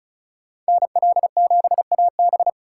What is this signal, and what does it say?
Morse code, heard as a keyed tone, about 34 wpm, 700 Hz. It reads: NL7AB
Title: USA Call Signs - 5 Characters - 4x Character Spacing - 35wpm